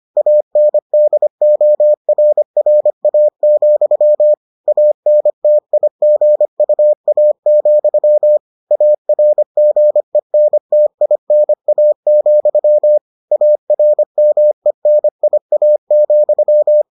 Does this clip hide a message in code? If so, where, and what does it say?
Yes: Morse code from 0.2 to 16.8 s, heard as a keyed tone, about 25 wpm, 600 Hz. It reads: ANDORRA, ANTIGUA, ARGENTINA, ARMENIA,